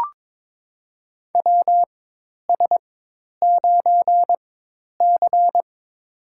Morse code, heard as a keyed tone, about 22 wpm, spaced as for 10 wpm, 700 Hz. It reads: WS9C
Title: USA Call Signs - 4 Characters - 4x Character Spacing - 22wpm